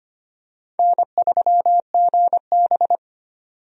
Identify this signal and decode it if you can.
Morse code, heard as a keyed tone, about 25 wpm, 700 Hz. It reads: N3GB
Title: USA Call Signs - 4 Characters - 25wpm